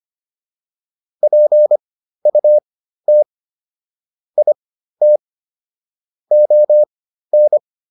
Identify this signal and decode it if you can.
Morse code, heard as a keyed tone, about 25 wpm, spaced as for 13 wpm, 600 Hz. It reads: PUT IT ON